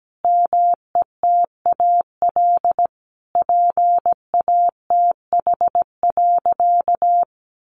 Morse code, heard as a keyed tone, about 17 wpm, 700 Hz. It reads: METAL PATH.
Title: A Sound Of Thunder 17wpm